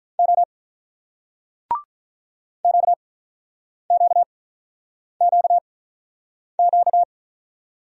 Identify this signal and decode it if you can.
Morse code, heard as a keyed tone, about 44 wpm, spaced as for 9 wpm, 700 Hz. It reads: O QQQQ